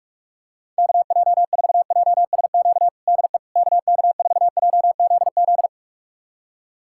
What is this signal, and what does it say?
Morse code, heard as a keyed tone, about 46 wpm, 700 Hz. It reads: KJ4JSQ DE KK4JZZ
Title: USA Callsign DE Callsign 45wpm